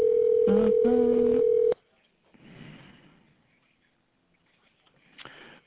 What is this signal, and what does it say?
Ringback tone (North America), 440 + 480 Hz.